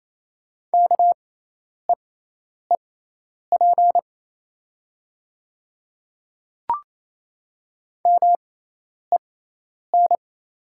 Morse code, heard as a keyed tone, about 28 wpm, spaced as for 10 wpm, 700 Hz. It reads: KEEP MEN
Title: 6x Character Spacing - Top 300 Words - 28wpm